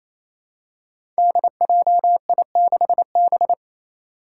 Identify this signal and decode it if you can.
Morse code, heard as a keyed tone, about 28 wpm, 700 Hz. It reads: DJI6B
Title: Five Challenging Letter-Number Combinations 28wpm